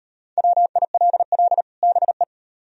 Morse code, heard as a keyed tone, about 37 wpm, 700 Hz. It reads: WILL BE